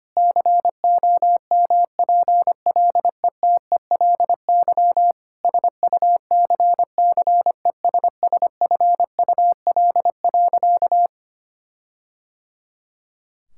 Morse code, heard as a keyed tone, about 25 wpm, 700 Hz. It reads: COMPLETELY SUCCESSFUL.
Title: Sentences from Top 1600 Words 25wpm